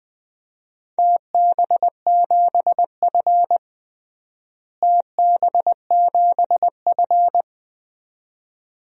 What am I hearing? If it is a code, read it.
Morse code, heard as a keyed tone, about 20 wpm, 700 Hz. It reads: TB7F TB7F
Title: DXCC-100 Call Signs - 4 Characters - Repeated 2x - 20wpm